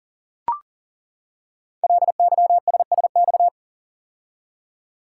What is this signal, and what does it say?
Morse code, heard as a keyed tone, about 42 wpm, 700 Hz. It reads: LYSSX
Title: Code Groups - 5 Letters - 40wpm